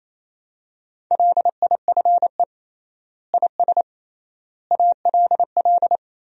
Morse code, heard as a keyed tone, about 28 wpm, 700 Hz. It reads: LIFE IS ALL
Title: 3x Word-Spacing - Sentences from Top 300 Words - 28wpm